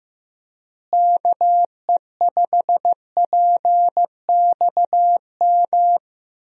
Morse code, heard as a keyed tone, about 15 wpm, 700 Hz. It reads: KE5PXM
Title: USA Call Signs - 6 Characters - Repeated 3x - 15wpm